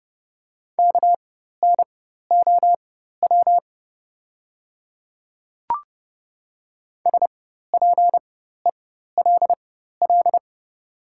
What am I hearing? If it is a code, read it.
Morse code, heard as a keyed tone, about 30 wpm, spaced as for 14 wpm, 700 Hz. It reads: KNOW SPELL
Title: 4x Character Spacing - Top 300 Words - 30wpm